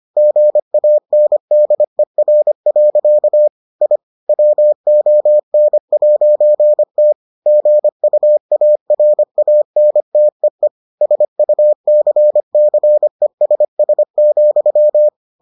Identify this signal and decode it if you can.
Morse code, heard as a keyed tone, about 25 wpm, 600 Hz. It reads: GANDER. I WON'T GUARANTEE SUCCESS,